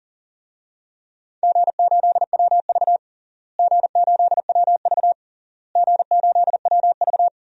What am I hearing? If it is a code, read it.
Morse code, heard as a keyed tone, about 40 wpm, 700 Hz. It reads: G8WV G8WV G8WV
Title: DXCC-100 Call Signs - 4 Characters - Repeated 3x - 40wpm